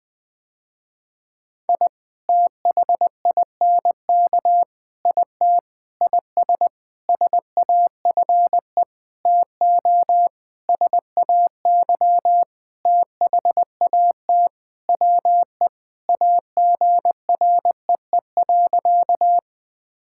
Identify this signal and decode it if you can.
Morse code, heard as a keyed tone, about 21 wpm, 700 Hz. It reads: I THINK IT IS SAFE TO SAY THAT WE AGREE.
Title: Sentences with 8-Word Phrases - Spoken Beginning - 20wpm